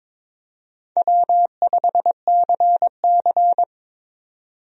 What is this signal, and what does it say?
Morse code, heard as a keyed tone, about 22 wpm, 700 Hz. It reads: W5CC
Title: USA Call Signs - 4 Characters - 22wpm